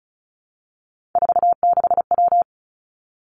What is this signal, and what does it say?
Morse code, heard as a keyed tone, about 36 wpm, 700 Hz. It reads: 46W
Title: Three Challenging Letter-Number Combinations 35wpm